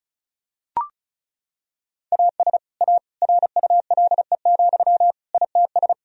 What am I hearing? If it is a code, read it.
Morse code, heard as a keyed tone, about 34 wpm, 700 Hz. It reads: AS A RULE, ITS
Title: Sentences from Top 1000 Words - Review Part 2 - 35wpm